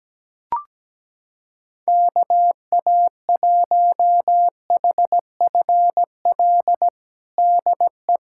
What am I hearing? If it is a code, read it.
Morse code, heard as a keyed tone, about 17 wpm, 700 Hz. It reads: KA1HFL DE
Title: USA Callsign DE Callsign 17wpm